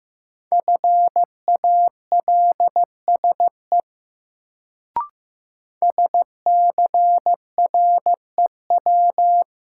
Morse code, heard as a keyed tone, about 15 wpm, 700 Hz. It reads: FALSE SCREW
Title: Sentences from Top 1600 Words - New Word Drill Alt - 15wpm